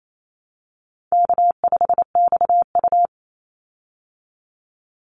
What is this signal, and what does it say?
Morse code, heard as a keyed tone, about 28 wpm, 700 Hz. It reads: K5XU
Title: Call Signs Repeated 3x - 28wpm